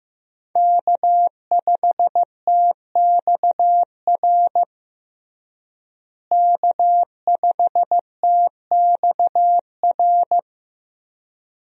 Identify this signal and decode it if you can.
Morse code, heard as a keyed tone, about 16 wpm, 700 Hz. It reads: K5TXR K5TXR
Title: USA Call Signs - 5 Characters - Repeated 2x - 15wpm